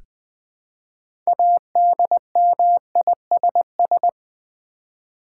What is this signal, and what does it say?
Morse code, heard as a keyed tone, about 20 wpm, 700 Hz. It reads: ADMISS
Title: Build A Word - 3 Letters At A Time - 9-Letter Words - 20wpm